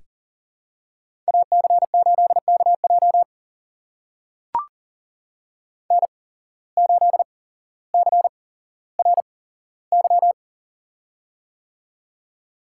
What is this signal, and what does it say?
Morse code, heard as a keyed tone, about 42 wpm, spaced as for 17 wpm, 700 Hz. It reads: AC8KJ N8CRY
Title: USA Call Signs - 8x Character Spacing - 40wpm